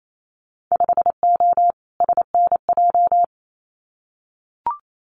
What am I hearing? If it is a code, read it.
Morse code, heard as a keyed tone, about 28 wpm, 700 Hz. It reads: 5O SNJ
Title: Winter Field Day Exchanges Encore 28wpm